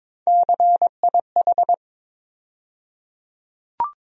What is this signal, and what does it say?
Morse code, heard as a keyed tone, about 22 wpm, 700 Hz. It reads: CIH
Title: USA Call Sign Postfixes - 3 Letters 22wpm